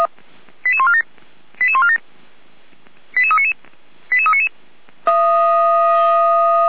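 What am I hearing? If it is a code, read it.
Five-tone selective call (ZVEI), heard as quick runs of five tones, tones of about 70 ms. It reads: 88147 88147 88399 88399
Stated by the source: ZVEI five-tone selective call received by SDR.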